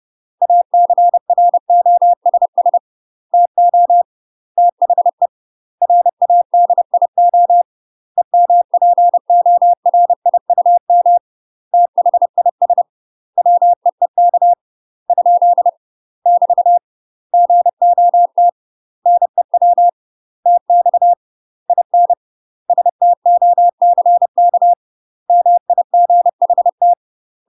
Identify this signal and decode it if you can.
Morse code, heard as a keyed tone, about 30 wpm, 700 Hz. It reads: ACROSS TO THE RADIO EMPORIUM THIS WEEK ? <BT> GOT NEW TX IN STOCK MIGHT
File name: UK MASTER QSOs TRAINING / 30WPM / UK MASTER QSOs 0021